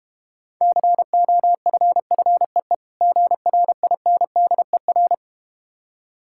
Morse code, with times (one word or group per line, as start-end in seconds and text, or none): 0.6-2.8 COFFEE
3.0-5.1 GRINDER